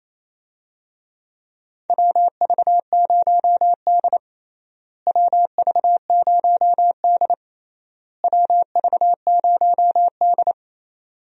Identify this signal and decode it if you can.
Morse code, heard as a keyed tone, about 28 wpm, 700 Hz. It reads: WV0D WV0D WV0D
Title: USA Call Signs - 4 Characters - Repeated 3x - 28wpm